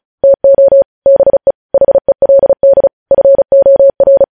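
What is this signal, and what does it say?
Morse code, heard as a keyed tone, about 36 wpm, 550 Hz. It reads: TO BE HELD FOR
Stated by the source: Morse code practice of news headlines at 35 wpm, opening with vvv.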